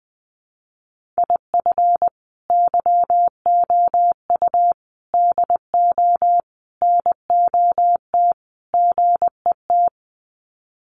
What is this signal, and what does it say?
Morse code, heard as a keyed tone, about 20 wpm, 700 Hz. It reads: IF YOU DO NOT GET
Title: Sets of 5 Words Encore2 20wpm